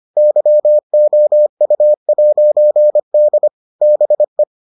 Morse code, heard as a keyed tone, about 25 wpm, 600 Hz. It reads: YOU'D BE